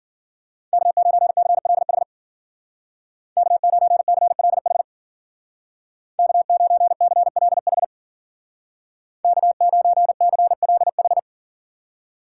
Morse code, heard as a keyed tone, about 52 wpm, 700 Hz. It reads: K9CLH K9CLH K9CLH K9CLH